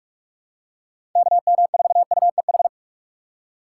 Morse code, heard as a keyed tone, about 44 wpm, 700 Hz. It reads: KM4UEH
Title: USA Call Signs - 6 Characters - 2x Character Spacing - 45wpm